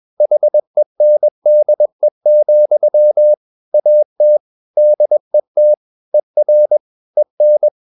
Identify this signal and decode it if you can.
Morse code, heard as a keyed tone, about 21 wpm, 600 Hz. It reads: HENDE, AT DET ER EN
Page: Ferie 20 wpm